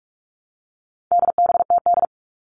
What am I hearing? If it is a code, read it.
Morse code, heard as a keyed tone, about 44 wpm, 700 Hz. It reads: DBTD
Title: All Cut Numbers - 4-Digit 45wpm